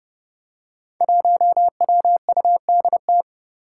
Morse code, heard as a keyed tone, about 29 wpm, 700 Hz. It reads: 1WUDT
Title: Five Challenging Letter-Number Combinations 30wpm